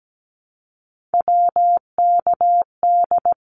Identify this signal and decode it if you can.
Morse code, heard as a keyed tone, about 17 wpm, 700 Hz. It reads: WKD
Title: Abbreviations Repeated 3x - 17wpm